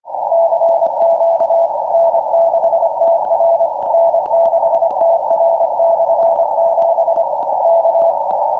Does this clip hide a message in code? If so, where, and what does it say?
Yes: Morse code from 0.3 to 8.1 s, heard as a keyed tone, about 25 wpm, 670 Hz, with noise. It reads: CQ N6TR N6TR/7 K